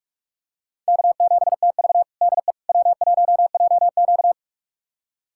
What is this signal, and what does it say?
Morse code, heard as a keyed tone, about 46 wpm, 700 Hz. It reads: K7TV DE W1JQ
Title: USA Callsign DE Callsign - 4 Characters - 45wpm